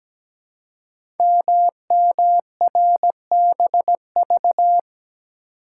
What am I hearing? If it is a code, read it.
Morse code, heard as a keyed tone, about 17 wpm, 700 Hz. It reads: MMRBV
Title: Code Groups - 5 Letters - 17wpm